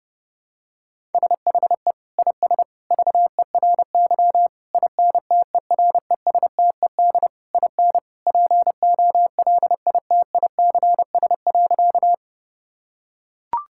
Morse code, heard as a keyed tone, about 30 wpm, 700 Hz. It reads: SHE IS VERY INTERESTED IN POLITICS.
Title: Sentences from Top 1000 Words - Review Part 6 - 30wpm